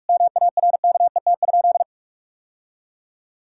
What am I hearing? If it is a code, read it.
Morse code, heard as a keyed tone, about 46 wpm, 700 Hz. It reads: MARKET?
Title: Sentences from Top 1000 Words - Review Part 6 - 45wpm